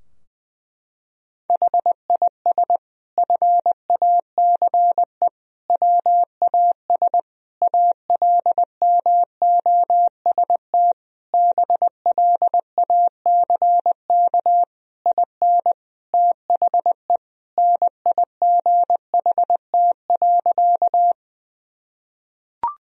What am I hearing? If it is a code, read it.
Morse code, heard as a keyed tone, about 21 wpm, 700 Hz. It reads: HIS FACE WAS ALMOST BLACK IN THE NIGHT.
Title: Sentences from Top 1000 Words - Review Part 4 - 20wpm